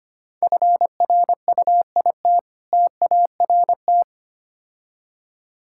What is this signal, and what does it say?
Morse code, heard as a keyed tone, about 25 wpm, 700 Hz. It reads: FRUIT TART
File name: practice / 25wpm / Difficult Words with Hints - 25wpm